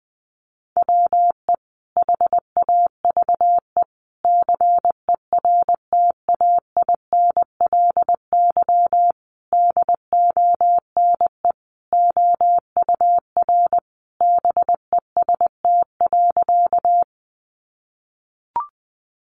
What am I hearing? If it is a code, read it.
Morse code, heard as a keyed tone, about 20 wpm, 700 Hz. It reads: WE HAVE CERTAINLY DONE OUR BEST.
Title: Sentences from Top 900 Words 20wpm